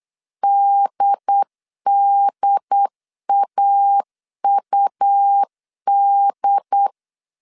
Morse code, heard as a keyed tone, about 8 wpm, 800 Hz. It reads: DDAUD
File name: CW / DET / 1 3 1500C S C 8wpm 4214